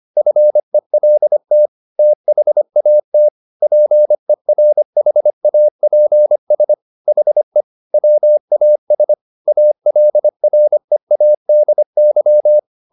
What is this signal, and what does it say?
Morse code, heard as a keyed tone, about 25 wpm, 600 Hz. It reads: FELT THAT PERHAPS HE WAS ALREADY